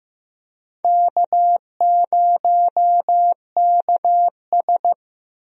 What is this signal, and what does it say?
Morse code, heard as a keyed tone, about 15 wpm, 700 Hz. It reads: K0KS
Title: USA Call Signs - 4 Characters - Repeated 2x - 15wpm